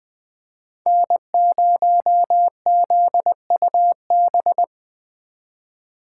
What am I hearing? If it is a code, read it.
Morse code, heard as a keyed tone, about 24 wpm, 700 Hz. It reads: N0ZUB
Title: Speed Racing - USA 1x3 Call Signs 20wpm